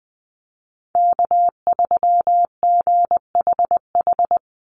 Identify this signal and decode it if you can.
Morse code, heard as a keyed tone, about 20 wpm, 700 Hz. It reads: K3GHH
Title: Call Signs - 20wpm